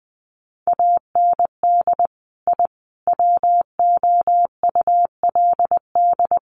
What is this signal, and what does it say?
Morse code, heard as a keyed tone, about 20 wpm, 700 Hz. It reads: AND I WOULD
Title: Sets of 5 Words Encore 20wpm